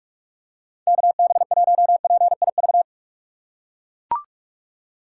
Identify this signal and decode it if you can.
Morse code, heard as a keyed tone, about 44 wpm, 700 Hz. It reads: KB1PIV
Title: USA Call Signs - 6 Characters - 6x Character Spacing - 45wpm